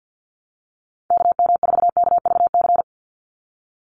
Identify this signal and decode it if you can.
Morse code, heard as a keyed tone, about 50 wpm, 700 Hz. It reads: KM4KVC
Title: USA Call Signs 50wpm